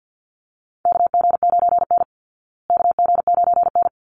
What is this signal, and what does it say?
Morse code, heard as a keyed tone, about 48 wpm, 700 Hz. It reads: KG9N KG9N
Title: Call Signs Repeated 3x - 50wpm